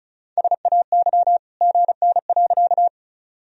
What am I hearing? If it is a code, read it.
Morse code, heard as a keyed tone, about 36 wpm, 700 Hz. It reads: SAY GN.
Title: Sentences from Top 300 Words and Abbreviations 35wpm